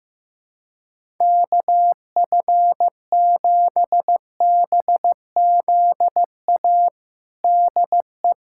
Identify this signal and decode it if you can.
Morse code, heard as a keyed tone, about 15 wpm, 700 Hz. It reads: KF7BZA DE
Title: USA Callsign DE Callsign - 6 Characters - 15wpm